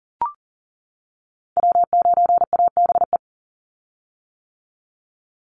Morse code, heard as a keyed tone, about 40 wpm, 700 Hz. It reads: W9ABE
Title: USA Call Signs 40wpm